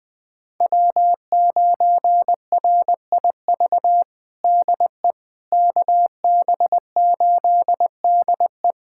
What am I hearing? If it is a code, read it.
Morse code, heard as a keyed tone, about 20 wpm, 700 Hz. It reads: W9RIV DE KB8DE
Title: USA Callsign DE Callsign - 5 Characters - 20wpm